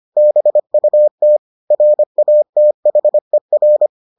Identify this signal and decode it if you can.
Morse code, heard as a keyed tone, about 25 wpm, 600 Hz. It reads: BUT RATHER